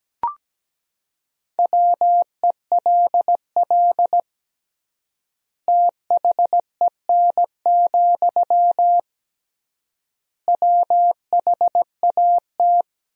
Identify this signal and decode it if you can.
Morse code, heard as a keyed tone, about 17 wpm, 700 Hz. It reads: WELL THEN, WHAT
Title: 3x Word-Spacing - Sentences from Top 200 Words - 17wpm